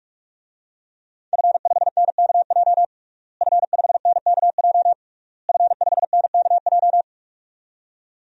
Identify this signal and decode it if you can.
Morse code, heard as a keyed tone, about 46 wpm, 700 Hz. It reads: F5NKJ F5NKJ F5NKJ
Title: DXCC-100 Call Signs - 5 Characters - Repeated 3x - 45wpm